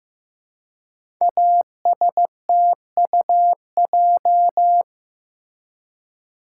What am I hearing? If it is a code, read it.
Morse code, heard as a keyed tone, about 15 wpm, 700 Hz. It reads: ASTUJ
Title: Code Groups - 5 Letters - 15wpm